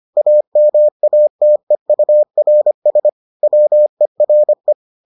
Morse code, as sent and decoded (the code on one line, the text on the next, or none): .- -- .- - . ..- .-. ... / .-- . .-. .
AMATEURS WERE